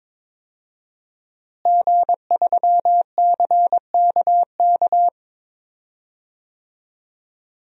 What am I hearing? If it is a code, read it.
Morse code, heard as a keyed tone, about 22 wpm, 700 Hz. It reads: G3CKK
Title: DXCC-100 Call Signs - 5 Characters - Repeated 2x - 22wpm